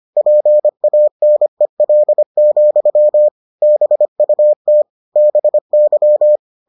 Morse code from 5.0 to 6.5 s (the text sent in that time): BY